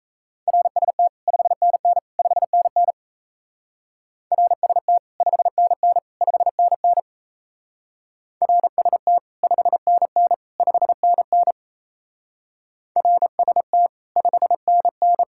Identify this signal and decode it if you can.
Morse code, heard as a keyed tone, about 33 wpm, 700 Hz. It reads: RST 5NN 5NN RST 5NN 5NN RST 5NN 5NN RST 5NN
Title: Speed-Racing Common QSO Phrases - 28wpm